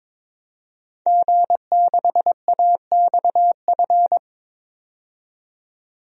Morse code, heard as a keyed tone, about 22 wpm, 700 Hz. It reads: G6AXF
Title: DXCC-100 Call Signs - 5 Characters - 22wpm